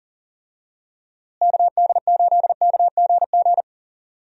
Morse code, heard as a keyed tone, about 40 wpm, 700 Hz. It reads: KD8KGG
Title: USA Call Signs - 6 Characters - Repeated 3x - 40wpm